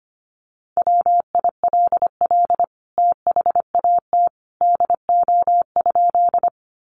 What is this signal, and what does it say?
Morse code, heard as a keyed tone, about 25 wpm, 700 Hz. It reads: WILL THAT DO?
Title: Sentences from Top 100 Words 25wpm